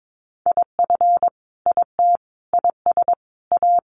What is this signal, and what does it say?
Morse code, heard as a keyed tone, about 22 wpm, 700 Hz. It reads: IF IT IS A
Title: Sentences from Top 1500 Words 22wpm